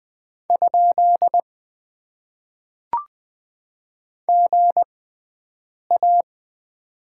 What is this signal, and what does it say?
Morse code, heard as a keyed tone, about 20 wpm, spaced as for 7 wpm, 700 Hz. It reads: ? GA
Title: Top 90 QSO Elements - 6x Character Spacing - 20wpm